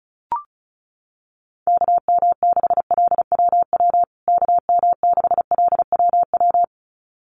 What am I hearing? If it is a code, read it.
Morse code, heard as a keyed tone, about 34 wpm, 700 Hz. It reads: KM6LWW KM6LWW
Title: USA Call Signs Repeated 2x 35wpm